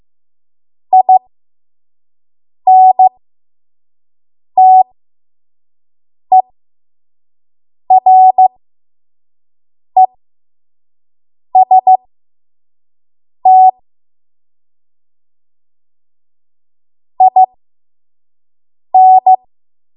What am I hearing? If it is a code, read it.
Morse code, heard as a keyed tone, about 15 wpm, spaced as for 5 wpm, 750 Hz. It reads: INTEREST IN
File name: Morse / 05WPM